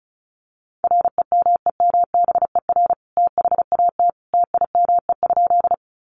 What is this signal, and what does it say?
Morse code, heard as a keyed tone, about 34 wpm, 700 Hz. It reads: REMEMBER THAT TIME?
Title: Sentences from Top 400 Words 35wpm